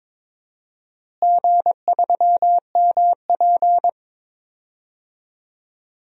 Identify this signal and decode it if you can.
Morse code, heard as a keyed tone, about 22 wpm, 700 Hz. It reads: G3MP
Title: DXCC-100 Call Signs - 4 Characters - Repeated 2x - 22wpm